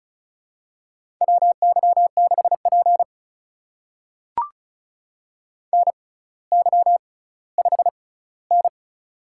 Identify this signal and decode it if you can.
Morse code, heard as a keyed tone, about 36 wpm, spaced as for 18 wpm, 700 Hz. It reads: WY6P NY5N
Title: USA Call Signs - 4 Characters - 6x Character Spacing - 35wpm